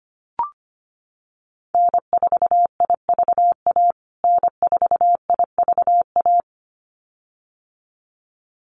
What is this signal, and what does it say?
Morse code, heard as a keyed tone, about 25 wpm, 700 Hz. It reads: N4IVA N4IVA
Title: USA Call Signs Repeated 2x 25wpm